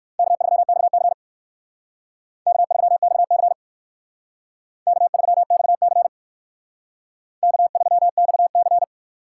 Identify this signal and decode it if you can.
Morse code, heard as a keyed tone, about 58 wpm, 700 Hz. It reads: K3XC K3XC K3XC K3XC